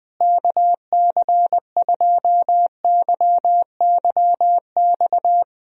Morse code, heard as a keyed tone, about 20 wpm, 700 Hz. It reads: KC2YYX
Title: USA Call Signs - 6 Characters - Repeated 2x - 20wpm